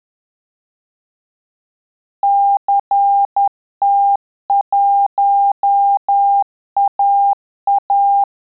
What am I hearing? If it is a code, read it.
Morse code, heard as a keyed tone, about 10 wpm, 800 Hz. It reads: CT1AA
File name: Telegraphy-group / QSO CW 50 CPM